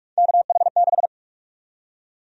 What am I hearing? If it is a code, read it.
Morse code, heard as a keyed tone, about 46 wpm, 700 Hz. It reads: KH6
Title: USA Call Sign Prefixes - 45wpm